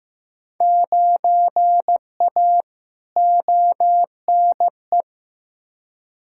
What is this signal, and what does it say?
Morse code, heard as a keyed tone, about 15 wpm, 700 Hz. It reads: 9A ONE
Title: Field Day Exchanges Encore - 15wpm